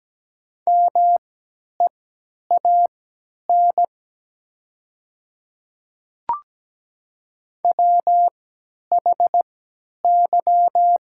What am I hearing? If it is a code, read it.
Morse code, heard as a keyed tone, about 17 wpm, spaced as for 10 wpm, 700 Hz. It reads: MEAN WHY
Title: 3x Character Spacing - Top 200 Words - 17wpm